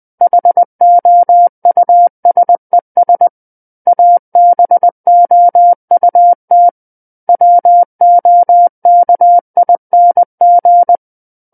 Morse code, heard as a keyed tone, about 20 wpm, 700 Hz. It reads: HOUSES ABOUT WOKING
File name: 20 WPM